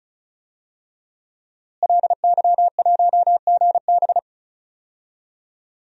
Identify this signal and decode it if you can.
Morse code, heard as a keyed tone, about 36 wpm, 700 Hz. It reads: LY1GB
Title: DXCC-100 Call Signs - 5 Characters - Repeated 3x - 35wpm